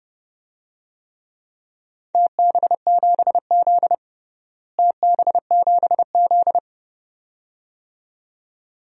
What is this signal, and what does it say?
Morse code, heard as a keyed tone, about 32 wpm, 700 Hz. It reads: TB7Z TB7Z
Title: DXCC-100 Call Signs - 4 Characters - Repeated 2x - 30wpm